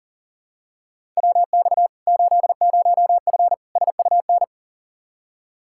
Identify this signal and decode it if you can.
Morse code, heard as a keyed tone, about 40 wpm, 700 Hz. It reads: WX 80F SUN
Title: Practice QSO - Exchange 2 - Info - 40wpm